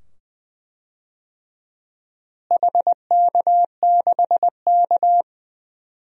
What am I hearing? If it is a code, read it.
Morse code, heard as a keyed tone, about 20 wpm, 700 Hz. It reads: HK6K
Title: DXCC-100 Call Signs - 4 Characters - Repeated 2x - 20wpm